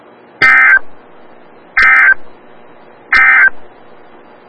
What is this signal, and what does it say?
Emergency Alert System (SAME) tones, heard as harsh buzzing bursts. It is heard from 0.3 to 3.6 s.
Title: Des Moines, IA NOAA Weather Radio - WXL57